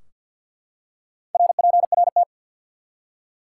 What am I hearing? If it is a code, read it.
Morse code, heard as a keyed tone, about 53 wpm, 700 Hz. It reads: RPRT
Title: Top 180 QSO Elements 50wpm